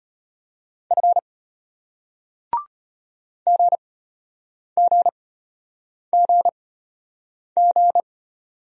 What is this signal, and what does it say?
Morse code, heard as a keyed tone, about 34 wpm, spaced as for 8 wpm, 700 Hz. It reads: F GGGG